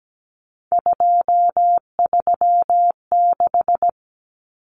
Morse code, heard as a keyed tone, about 17 wpm, 700 Hz. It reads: 236
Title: Numbers - 3-Digit 17wpm